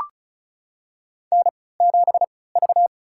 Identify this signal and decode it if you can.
Morse code, heard as a keyed tone, about 34 wpm, spaced as for 20 wpm, 700 Hz. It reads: N7V
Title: USA Call Signs - 3x Character Spacing - 35wpm